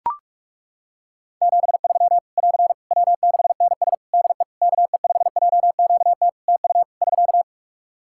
Morse code, heard as a keyed tone, about 41 wpm, 700 Hz. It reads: 73 <AR> W6NS DE KE5JQT TU <SK>
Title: Practice QSO - Exchange 4 - Ending - 45wpm